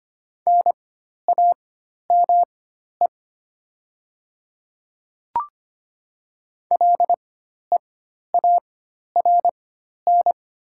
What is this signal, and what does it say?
Morse code, heard as a keyed tone, about 25 wpm, spaced as for 12 wpm, 700 Hz. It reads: NAME LEARN
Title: 4x Character Spacing - Top 500 Words - 25wpm